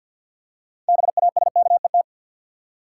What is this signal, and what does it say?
Morse code, heard as a keyed tone, about 52 wpm, 700 Hz. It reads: BASKET
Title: Six Letter Words - 50wpm